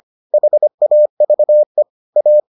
Morse code, heard as a keyed tone, about 25 wpm, 600 Hz. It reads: HAVE A